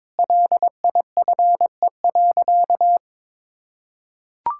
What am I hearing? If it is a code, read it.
Morse code, heard as a keyed tone, about 21 wpm, 700 Hz. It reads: LIFE.
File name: practice / 22wpm / Sentences with 8-Word Phrases - 22wpm